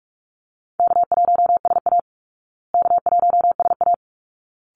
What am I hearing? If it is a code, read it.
Morse code, heard as a keyed tone, about 44 wpm, 700 Hz. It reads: K1SA K1SA
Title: Easy Call Signs Repeated 2x - 45wpm